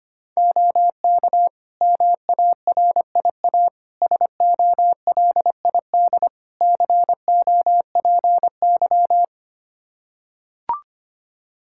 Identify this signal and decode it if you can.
Morse code, heard as a keyed tone, about 25 wpm, 700 Hz. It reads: OK MARIA SOLID COPY
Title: Practice QSO - Exchange 3 - More Info - 25wpm